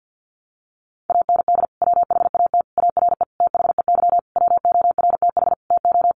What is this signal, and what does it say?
Morse code, heard as a keyed tone, about 50 wpm, 700 Hz. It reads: AND WHAT ARE THEY WORTH TO
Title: Sentences from Top 800 Words 50wpm